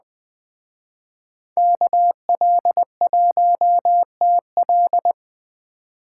Morse code, heard as a keyed tone, about 20 wpm, 700 Hz. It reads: KL1TL